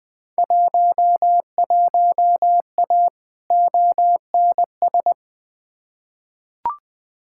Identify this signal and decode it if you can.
Morse code, heard as a keyed tone, about 20 wpm, 700 Hz. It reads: 11A ONS
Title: Field Day Exchanges Encore - 20wpm